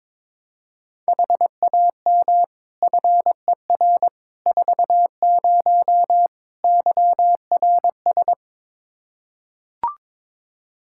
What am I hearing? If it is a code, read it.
Morse code, heard as a keyed tone, about 22 wpm, 700 Hz. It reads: HAM FER 40 YRS
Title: Practice QSO - Exchange 3 - More Info - 22wpm